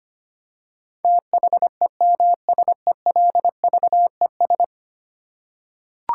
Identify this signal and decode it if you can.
Morse code, heard as a keyed tone, about 25 wpm, 700 Hz. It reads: THEMSELVES
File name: Ten Letter Words - 25wpm